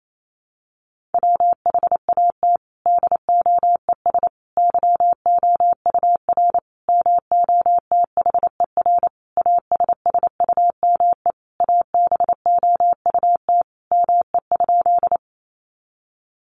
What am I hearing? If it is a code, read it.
Morse code, heard as a keyed tone, about 28 wpm, 700 Hz. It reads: WHAT DOES YOUR MOTHER ASSUME ABOUT ME?
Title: Sentences from Top 1500 Words 28wpm